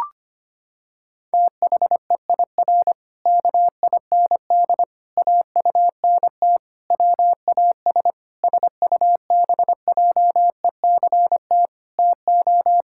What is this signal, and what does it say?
Morse code, heard as a keyed tone, about 25 wpm, 700 Hz. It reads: THEIR KIND AUNT WAS SUBJECT TO
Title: Pangram Challenge v2 - 25wpm